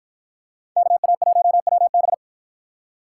Morse code, heard as a keyed tone, about 52 wpm, 700 Hz. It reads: KA1WB